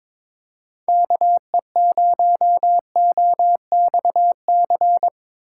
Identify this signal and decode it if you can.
Morse code, heard as a keyed tone, about 22 wpm, 700 Hz. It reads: KE0OXC
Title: USA Call Signs - 6 Characters - 2x Character Spacing - 22wpm